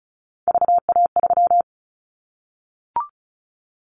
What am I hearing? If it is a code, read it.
Morse code, heard as a keyed tone, about 34 wpm, 700 Hz. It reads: VA3
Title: Callsign Prefixes for Top 100 DXCC - 35wpm